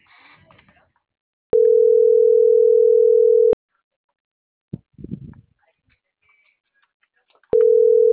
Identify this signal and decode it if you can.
Ringback tone (North America), 440 + 480 Hz.